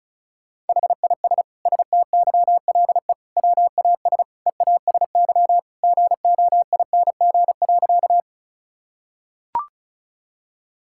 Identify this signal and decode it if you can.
Morse code, heard as a keyed tone, about 36 wpm, 700 Hz. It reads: HIS STYLE WAS EASY GOING.
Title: Sentences from Top 1000 Words - Review Part 5 - 35wpm